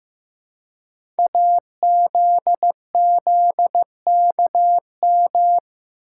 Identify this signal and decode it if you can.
Morse code, heard as a keyed tone, about 15 wpm, 700 Hz. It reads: AZZKM
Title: Code Groups - 5 Letters - 15wpm